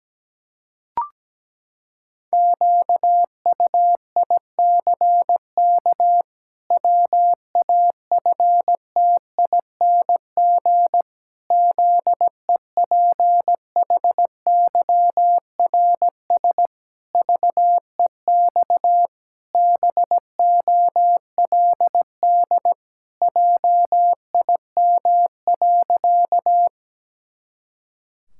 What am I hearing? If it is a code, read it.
Morse code, heard as a keyed tone, about 17 wpm, 700 Hz. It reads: QUICK WAFTING ZEPHYRS VEX BOLD JIM.
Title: Pangram Challenge v2 - 17wpm